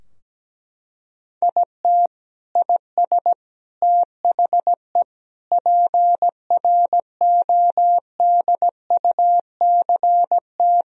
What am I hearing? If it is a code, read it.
Morse code, heard as a keyed tone, about 17 wpm, 700 Hz. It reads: IT IS THE PRODUCT
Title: Sentences from Top 1000 Words - Review Part 2 - 17wpm